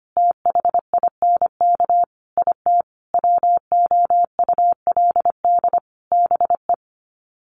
Morse code, heard as a keyed tone, about 25 wpm, 700 Hz. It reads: THINK IT WOULD BE
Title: Sets of 5 Words 25wpm